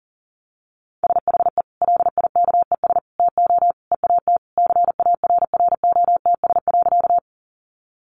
Morse code, heard as a keyed tone, about 40 wpm, 700 Hz. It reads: SHE LIKES TO EAT CARROTS.
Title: Easy Phrases 40wpm